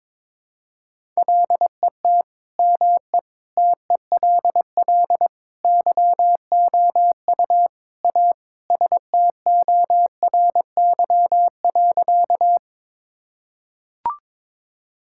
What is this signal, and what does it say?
Morse code, heard as a keyed tone, about 22 wpm, 700 Hz. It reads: LET ME TELL YOU A STORY.
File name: practice / 22wpm / Sentences with 4-Word Phrases - 22wpm